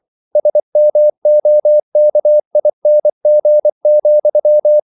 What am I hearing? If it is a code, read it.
Morse code, heard as a keyed tone, about 24 wpm, 600 Hz. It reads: SMOKING,